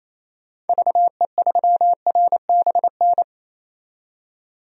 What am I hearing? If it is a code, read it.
Morse code, heard as a keyed tone, about 28 wpm, 700 Hz. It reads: VE3RBN
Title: DXCC-100 Call Signs - 6 Characters - 28wpm